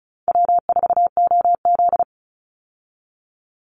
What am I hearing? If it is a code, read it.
Morse code, heard as a keyed tone, about 34 wpm, 700 Hz. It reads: W4OZ
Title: USA Call Signs 35wpm